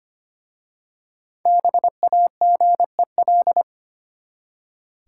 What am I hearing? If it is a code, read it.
Morse code, heard as a keyed tone, about 25 wpm, 700 Hz. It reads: BAGEL
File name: practice / 25wpm / Easy Words with Hints - 25wpm